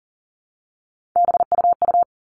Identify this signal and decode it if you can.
Morse code, heard as a keyed tone, about 40 wpm, 700 Hz. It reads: BUU
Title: All Cut Numbers - 3-Digit 40wpm